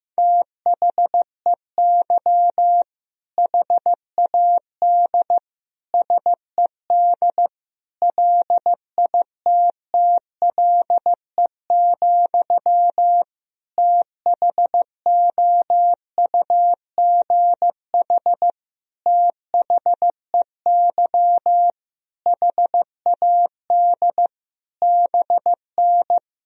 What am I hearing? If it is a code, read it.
Morse code, heard as a keyed tone, about 15 wpm, 700 Hz. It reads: THEY HAD SED LITTLE, THOUGH THEY HAD BN
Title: Sentences from Top 400 Words and Abbreviations 15wpm